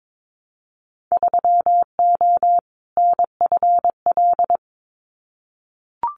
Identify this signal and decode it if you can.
Morse code, heard as a keyed tone, about 22 wpm, 700 Hz. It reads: 3O NFL
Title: Winter Field Day Exchanges 22wpm